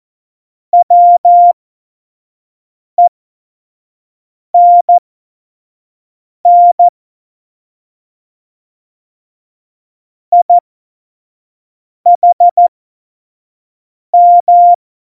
Morse code, heard as a keyed tone, about 14 wpm, spaced as for 5 wpm, 700 Hz. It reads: WENN IHM
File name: Morsekurs / Witze / Witz 11. September 2018, 25 F70